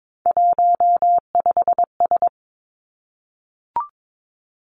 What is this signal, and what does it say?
Morse code, heard as a keyed tone, about 22 wpm, 700 Hz. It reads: 15S
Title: Three Challenging Letter-Number Combinations 22wpm